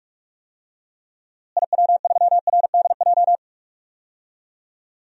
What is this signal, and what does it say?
Morse code, heard as a keyed tone, about 42 wpm, 700 Hz. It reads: IW3RDJ
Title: DXCC-100 Call Signs - 6 Characters - Repeated 2x - 45wpm